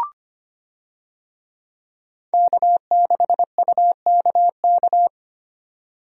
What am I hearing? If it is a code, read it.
Morse code, heard as a keyed tone, about 25 wpm, 700 Hz. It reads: K6UKK
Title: USA Call Signs - 5 Characters - 25wpm